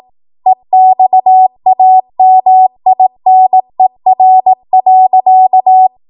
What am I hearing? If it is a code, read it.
Morse code, heard as a keyed tone, about 18 wpm, 750 Hz. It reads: EXAMINER.
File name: Archive / 18 WPM / 18WPM